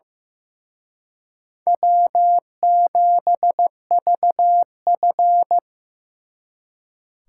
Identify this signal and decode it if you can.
Morse code, heard as a keyed tone, about 18 wpm, 700 Hz. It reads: W7VF
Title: Speed Racing - USA Call Signs 15wpm